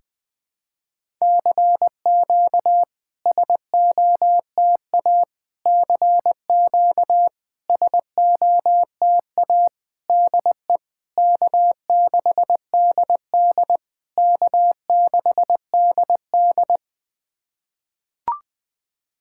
Morse code, heard as a keyed tone, about 20 wpm, 700 Hz. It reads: CQ SOTA CQ SOTA DE K6DD K6DD
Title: SOTA QSO Practice 20wpm